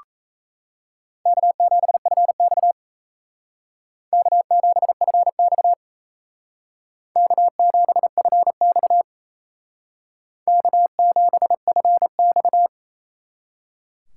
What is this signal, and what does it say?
Morse code, heard as a keyed tone, about 35 wpm, 700 Hz. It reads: K7FX K7FX K7FX K7FX